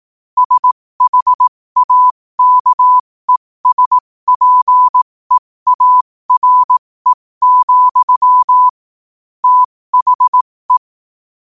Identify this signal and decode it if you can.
Morse code, heard as a keyed tone, about 18 wpm, spaced as for 15 wpm, 1000 Hz. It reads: SHAKESPEARE, THE
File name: qotdfiles / 15wpm